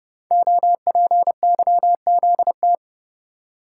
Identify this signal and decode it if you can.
Morse code, heard as a keyed tone, about 30 wpm, 700 Hz. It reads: OPYZT
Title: Code Groups - 5 Letters - 30wpm